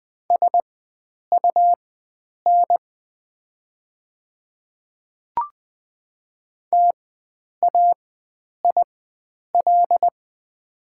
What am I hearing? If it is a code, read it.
Morse code, heard as a keyed tone, about 20 wpm, spaced as for 9 wpm, 700 Hz. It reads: SUN TAIL
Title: 4x Character Spacing - Top 500 Words - 20wpm